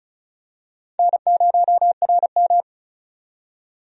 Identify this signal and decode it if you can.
Morse code, heard as a keyed tone, about 34 wpm, 700 Hz. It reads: N0RM
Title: USA Call Signs - 4 Characters - 35wpm